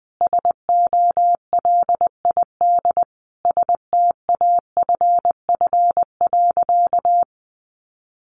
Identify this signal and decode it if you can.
Morse code, heard as a keyed tone, about 20 wpm, 700 Hz. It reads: SOLID STAFF.
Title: Sentences from Top 1500 Words 20wpm